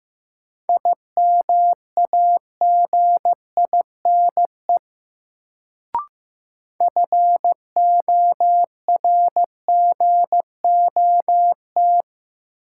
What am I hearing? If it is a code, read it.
Morse code, heard as a keyed tone, about 15 wpm, 700 Hz. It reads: IMAGINE FORGOT
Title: Sentences from Top 1400 Words - New Word Drill - 15wpm